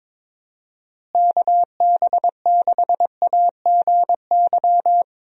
Morse code, heard as a keyed tone, about 22 wpm, 700 Hz. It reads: KB6AGY
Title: USA Call Signs - 6 Characters - Repeated 3x - 22wpm